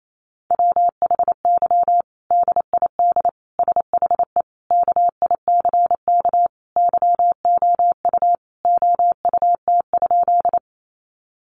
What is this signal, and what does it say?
Morse code, heard as a keyed tone, about 28 wpm, 700 Hz. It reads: WHY DID SHE KICK YOU OUT?
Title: Sentences from Top 1300 Words 28wpm